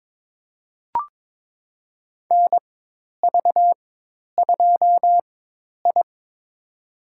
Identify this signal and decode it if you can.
Morse code, heard as a keyed tone, about 22 wpm, spaced as for 10 wpm, 700 Hz. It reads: NV2I
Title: USA Call Signs - 4x Character Spacing - 22wpm